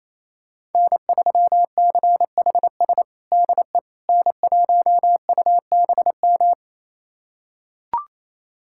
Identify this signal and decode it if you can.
Morse code, heard as a keyed tone, about 28 wpm, 700 Hz. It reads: N3CHS DE N1UBM
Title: USA Callsign DE Callsign - 5 Characters - 28wpm